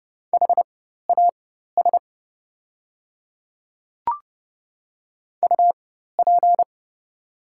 Morse code, heard as a keyed tone, about 30 wpm, spaced as for 14 wpm, 700 Hz. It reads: HAS UP
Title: 4x Character Spacing - Top 100 Words - 30wpm